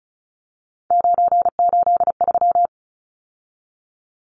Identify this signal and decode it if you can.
Morse code, heard as a keyed tone, about 34 wpm, 700 Hz. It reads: 983
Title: Numbers - 3-Digit 35wpm